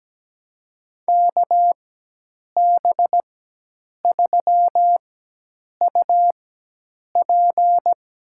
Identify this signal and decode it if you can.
Morse code, heard as a keyed tone, about 17 wpm, spaced as for 8 wpm, 700 Hz. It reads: KB3UP
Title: USA Call Signs - 4x Character Spacing - 17wpm